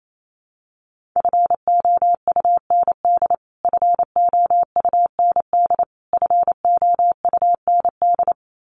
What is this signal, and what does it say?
Morse code, heard as a keyed tone, about 28 wpm, 700 Hz. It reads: FOUND FOUND FOUND
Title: Top 500 Words Repeated 28wpm